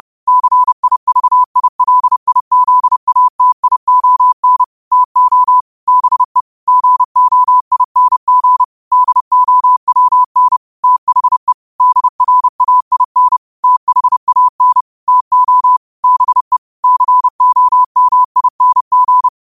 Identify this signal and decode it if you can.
Morse code, heard as a keyed tone, about 30 wpm, 1000 Hz. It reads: CIVILIZATION TO BE GOING DOWN THE DRAIN THAN TO BE COMING
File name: qotdfiles / 30wpm